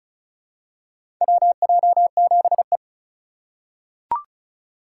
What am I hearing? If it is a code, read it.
Morse code, heard as a keyed tone, about 35 wpm, 700 Hz. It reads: WJ7E
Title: USA Call Signs - 4 Characters - 2x Character Spacing - 35wpm